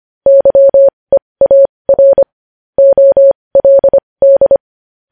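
Morse code, heard as a keyed tone, about 25 wpm, spaced as for 20 wpm, 550 Hz. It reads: YEAR OLD
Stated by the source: Morse code news headlines practice at 20 wpm.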